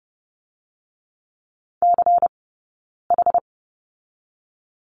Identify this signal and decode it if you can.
Morse code, heard as a keyed tone, about 30 wpm, 700 Hz. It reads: C H
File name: Lesson 19d - Letter w - Two Characters 30wpm